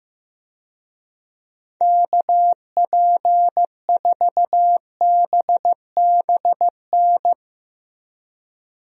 Morse code, heard as a keyed tone, about 16 wpm, 700 Hz. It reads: KP4BBN
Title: DXCC-100 Call Signs - 6 Characters - Repeated 2x - 15wpm